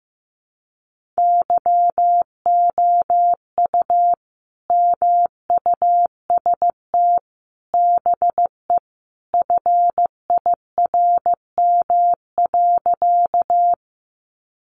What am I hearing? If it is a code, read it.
Morse code, heard as a keyed tone, about 15 wpm, 700 Hz. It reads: YOU MUST BE FIRM.
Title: Sentences from Top 1400 Words 15wpm